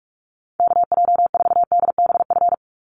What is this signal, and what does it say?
Morse code, heard as a keyed tone, about 44 wpm, 700 Hz. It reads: KJ4DBF
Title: USA Call Signs 45wpm